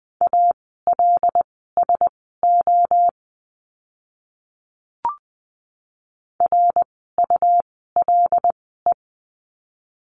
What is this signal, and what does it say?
Morse code, heard as a keyed tone, about 20 wpm, spaced as for 15 wpm, 700 Hz. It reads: ALSO RULE
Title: 101 Lesson 17f - Letter u - Words 20wpm